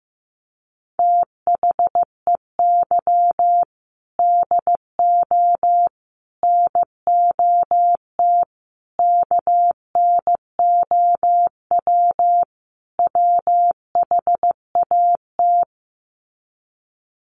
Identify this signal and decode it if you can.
Morse code, heard as a keyed tone, about 15 wpm, 700 Hz. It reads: THEY DO NOT KNOW WHAT
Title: Sets of 5 Words Encore 15wpm